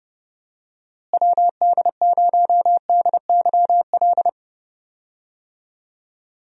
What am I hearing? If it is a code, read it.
Morse code, heard as a keyed tone, about 29 wpm, 700 Hz. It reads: WD0DYL
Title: USA Call Signs - 6 Characters - Repeated 2x - 30wpm